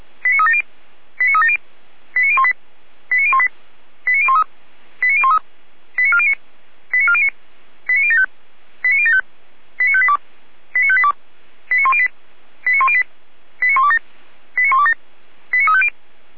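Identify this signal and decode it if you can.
Five-tone selective call (ZVEI), heard as quick runs of five tones, tones of about 70 ms. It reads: 89370 89370 89018 89018 89013 89013 89409 89409 89075 89075 89582 89582 89108 89108 89127 89127 89350
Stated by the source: ZVEI five-tone selective call 89500, received by SDR, from the audio archive of Feuerwehr Orlamünde.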